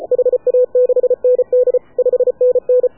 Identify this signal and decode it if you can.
Morse code, heard as a keyed tone, about 33 wpm, 500 Hz. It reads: HA6ND 5NN
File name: AUDIO2019 / HA7UI CW